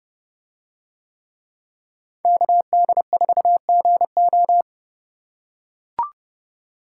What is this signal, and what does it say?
Morse code, heard as a keyed tone, about 29 wpm, 700 Hz. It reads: KD4GO